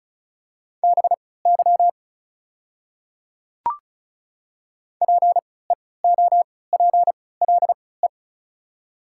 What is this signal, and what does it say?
Morse code, heard as a keyed tone, about 36 wpm, spaced as for 20 wpm, 700 Hz. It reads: BY PEOPLE
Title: 3x Character Spacing - Top 200 Words - 35wpm